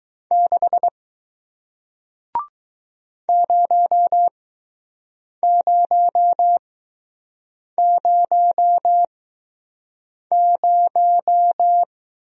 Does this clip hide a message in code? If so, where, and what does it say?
Yes: Morse code from 0.3 to 11.8 s, heard as a keyed tone, about 20 wpm, 700 Hz. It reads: 6 0 0 0 0